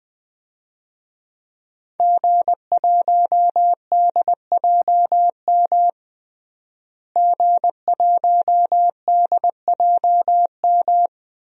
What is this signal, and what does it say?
Morse code, heard as a keyed tone, about 20 wpm, 700 Hz. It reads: G1DJM G1DJM
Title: DXCC-100 Call Signs - 5 Characters - Repeated 2x - 20wpm